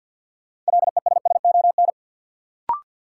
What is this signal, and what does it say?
Morse code, heard as a keyed tone, about 53 wpm, 700 Hz. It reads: LESSON
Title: Sentences from Top 1200 Words - New Word Drill Alt - 50wpm